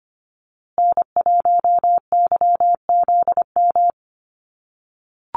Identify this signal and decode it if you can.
Morse code, heard as a keyed tone, about 25 wpm, 700 Hz. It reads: N1YZM
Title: USA Call Signs Repeated 3x 25wpm